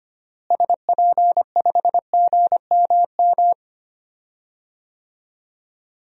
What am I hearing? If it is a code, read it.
Morse code, heard as a keyed tone, about 25 wpm, 700 Hz. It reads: SP5GMM
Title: DXCC-100 Call Signs - 6 Characters - Repeated 2x - 25wpm